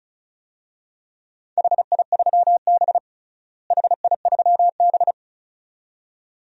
Morse code, heard as a keyed tone, about 34 wpm, 700 Hz. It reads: HI3B HI3B
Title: DXCC-100 Call Signs - 4 Characters - Repeated 2x - 35wpm